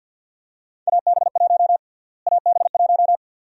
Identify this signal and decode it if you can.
Morse code, heard as a keyed tone, about 50 wpm, 700 Hz. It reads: AB1 AB1
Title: USA Call Sign Prefixes - Repeated 2x - 50wpm